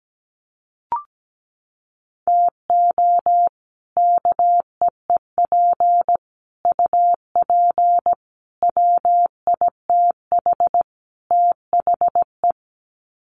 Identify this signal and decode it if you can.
Morse code, heard as a keyed tone, about 17 wpm, 700 Hz. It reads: TO KEEP UP WITH THE
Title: Sets of 5 Words Encore 17wpm